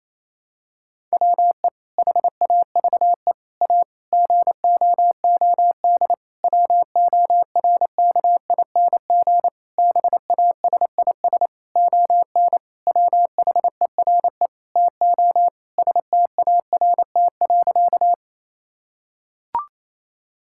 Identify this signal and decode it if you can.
Morse code, heard as a keyed tone, about 28 wpm, 700 Hz. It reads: WE HAVE A GOOD WORKING BASIS ON WHERE TO START.
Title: Sentences from Top 1000 Words - Review Part 5 - 28wpm